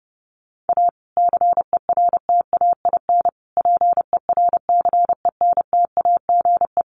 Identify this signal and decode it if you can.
Morse code, heard as a keyed tone, about 30 wpm, 700 Hz. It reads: A CERTAIN PERCENTAGE
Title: Sentences from Top 1300 Words 30wpm